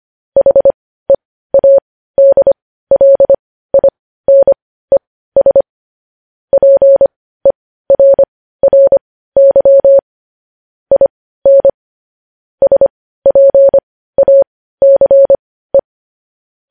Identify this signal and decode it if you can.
Morse code, heard as a keyed tone, about 25 wpm, spaced as for 15 wpm, 550 Hz. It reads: HEADLINES PERRY IN SPACE
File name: podcast / News headlines-1x-25x15